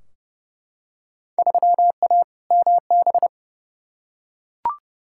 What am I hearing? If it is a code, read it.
Morse code, heard as a keyed tone, about 30 wpm, 700 Hz. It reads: 3A MB
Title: Field Day Exchanges Encore - 30wpm